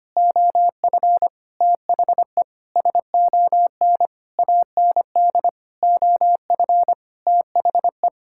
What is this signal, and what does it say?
Morse code, heard as a keyed tone, about 25 wpm, 700 Hz. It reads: OF THE SON AND OF THE
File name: practice / 25wpm / Sentences with 9-Word Phrases - 25wpm